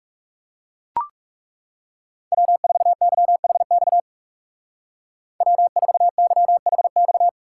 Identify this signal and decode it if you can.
Morse code, heard as a keyed tone, about 37 wpm, 700 Hz. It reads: W4YHX W4YHX
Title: Speed Racing - USA Call Signs 30wpm